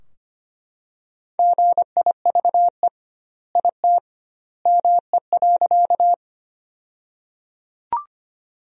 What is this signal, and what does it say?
Morse code, heard as a keyed tone, about 25 wpm, 700 Hz. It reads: GIVE IT ME.
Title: 2x Word-Spacing - Sentences from Top 200 Words - 25wpm